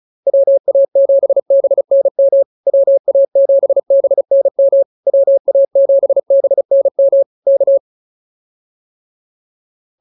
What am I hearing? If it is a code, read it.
Morse code, heard as a keyed tone, about 34 wpm, 550 Hz. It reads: WA7BNM WA7BNM WA7BNM K